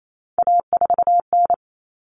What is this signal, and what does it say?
Morse code, heard as a keyed tone, about 28 wpm, 700 Hz. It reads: A4N
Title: Common Cut Numbers - 3-Digit 28wpm